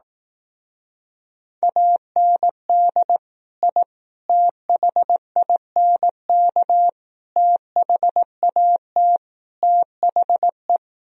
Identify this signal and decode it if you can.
Morse code, heard as a keyed tone, about 18 wpm, 700 Hz. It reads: AND I THINK THAT THE
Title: Speed-Racing - Sets of 5 Words 15wpm